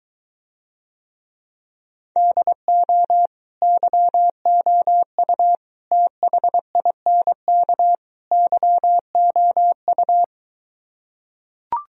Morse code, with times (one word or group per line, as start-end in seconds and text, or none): 2.2-3.3 DO
3.6-5.6 YOU
5.9-8.0 THINK
8.3-10.2 YOU